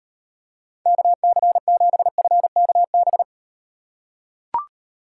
Morse code, heard as a keyed tone, about 38 wpm, 700 Hz. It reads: KC7FKB